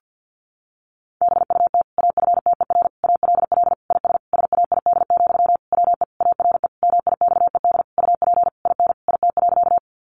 Morse code, heard as a keyed tone, about 50 wpm, 700 Hz. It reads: BUT AFTER ALL IS SAID, WE ARE MIXED UP IN IT.
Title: Sentences from Top 1500 Words 50wpm